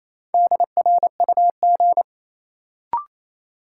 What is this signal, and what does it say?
Morse code, heard as a keyed tone, about 28 wpm, 700 Hz. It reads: DRUG
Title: Sets of 5 Words - New Word Drill - 28wpm